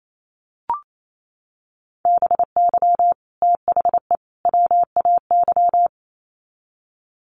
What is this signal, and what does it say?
Morse code, heard as a keyed tone, about 28 wpm, 700 Hz. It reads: BY THE WAY
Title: Sets of 3 Words 28wpm